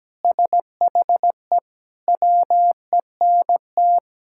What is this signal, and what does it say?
Morse code, heard as a keyed tone, about 17 wpm, 700 Hz. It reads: SHE WENT
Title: Sentences from Top 1000 Words - Review Part 6 - 17wpm